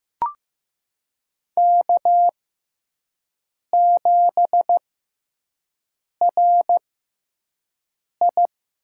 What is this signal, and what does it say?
Morse code, heard as a keyed tone, about 15 wpm, spaced as for 5 wpm, 700 Hz. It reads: K7RI
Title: USA Call Signs - 5 Characters - 6x Character Spacing - 15wpm